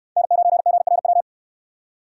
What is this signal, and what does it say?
Morse code, heard as a keyed tone, about 69 wpm, 700 Hz. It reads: N9GRW